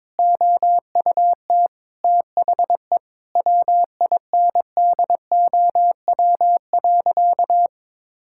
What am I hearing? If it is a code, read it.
Morse code, heard as a keyed tone, about 21 wpm, 700 Hz. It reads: OUT THE WINDOW.
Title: Sentences from Top 1600 Words 22wpm